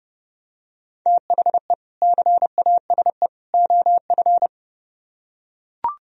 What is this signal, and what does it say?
Morse code, heard as a keyed tone, about 29 wpm, 700 Hz. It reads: THE CASE OF